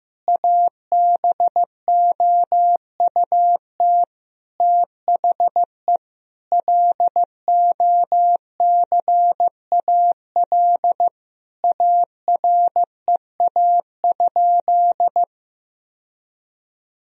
Morse code, heard as a keyed tone, about 15 wpm, 700 Hz. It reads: ABOUT THE LOCAL AREA?
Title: Sentences from Top 1000 Words - Review Part 3 - 15wpm